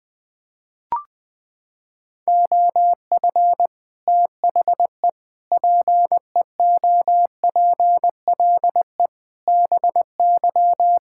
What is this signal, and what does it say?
Morse code, heard as a keyed tone, about 20 wpm, 700 Hz. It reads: OF THE PEOPLE BY
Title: Sets of 8 Words - 20wpm